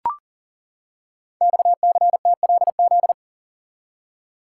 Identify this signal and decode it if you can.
Morse code, heard as a keyed tone, about 39 wpm, 700 Hz. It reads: XCTLZ
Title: Code Groups - 5 Letters - 40wpm